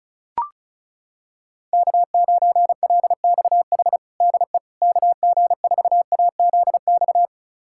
Morse code, heard as a keyed tone, about 36 wpm, 700 Hz. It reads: K9LXH DE KG4AZX
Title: USA Callsign DE Callsign 35wpm